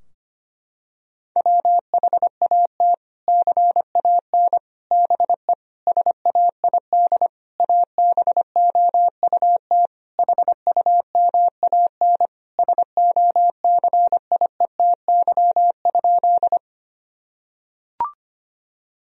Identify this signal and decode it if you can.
Morse code, heard as a keyed tone, about 25 wpm, 700 Hz. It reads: WHAT CAN BE SAID ABOUT HUMAN SOCIETY?
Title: Sentences from Top 1000 Words - Review Part 4 - 25wpm